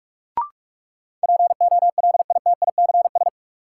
Morse code, heard as a keyed tone, about 46 wpm, 700 Hz. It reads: POLITICS
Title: Sentences from Top 1000 Words - New Word Drill - 45wpm